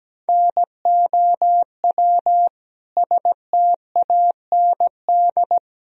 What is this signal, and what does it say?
Morse code, heard as a keyed tone, about 17 wpm, 700 Hz. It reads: NOW STAND
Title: Sentences from Top 1000 Words - Review Part 1 - 17wpm